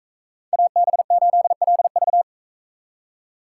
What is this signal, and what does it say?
Morse code, heard as a keyed tone, about 40 wpm, 700 Hz. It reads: AB8LV